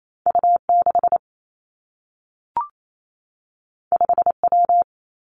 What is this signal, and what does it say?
Morse code, heard as a keyed tone, about 28 wpm, 700 Hz. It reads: U 6 5 W
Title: Two Challenging Letter-Number Combinations 28wpm